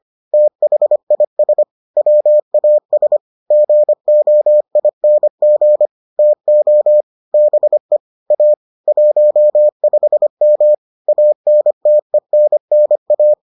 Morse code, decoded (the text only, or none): THIS WAS GOING TO BE A 15M ANTENNA